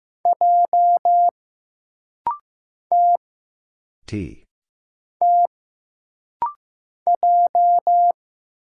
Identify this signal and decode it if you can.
Morse code, heard as a keyed tone, about 15 wpm, 700 Hz. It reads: J T T J
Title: Single Letter-Number Alt - Rapid-Fire 15wpm